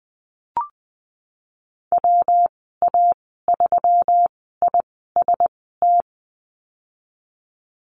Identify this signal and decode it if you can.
Morse code, heard as a keyed tone, about 19 wpm, spaced as for 14 wpm, 700 Hz. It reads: WA3IST
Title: 128 Lesson 21g - Number 6 - Call Signs 20wpm